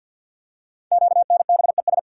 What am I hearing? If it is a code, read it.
Morse code, heard as a keyed tone, about 46 wpm, 700 Hz. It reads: QNBES
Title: Code Groups - 5 Letters - 50wpm